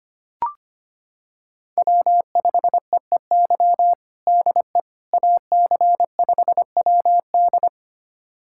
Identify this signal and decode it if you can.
Morse code, heard as a keyed tone, about 25 wpm, 700 Hz. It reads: W5EEY DE AC5WD
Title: USA Callsign DE Callsign - 5 Characters - 25wpm